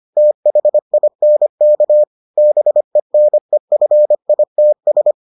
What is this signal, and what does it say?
Morse code, heard as a keyed tone, about 25 wpm, 600 Hz. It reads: THINK BENEFITS